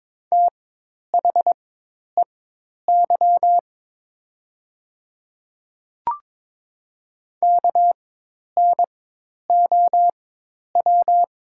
Morse code, heard as a keyed tone, about 22 wpm, spaced as for 10 wpm, 690 Hz. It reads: THEY KNOW
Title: 4x Character Spacing - Top 300 Words - 22wpm